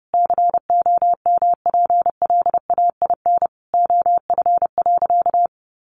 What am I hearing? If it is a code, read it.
Morse code, heard as a keyed tone, about 30 wpm, 700 Hz. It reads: COMPLAIN OF.
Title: Sentences from Top 1300 Words 30wpm